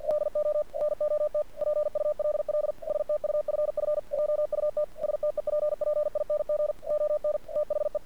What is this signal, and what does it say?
Morse code, heard as a keyed tone, about 47 wpm, 600 Hz. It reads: DO NOT PULL STUFF JUT STEPPING ON THE
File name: archived / CRSnet